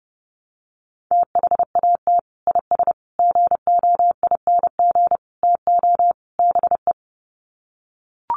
Morse code, heard as a keyed tone, about 28 wpm, 700 Hz. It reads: THAT IS GOING TO BE
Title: Sets of 5 Words Encore2 30wpm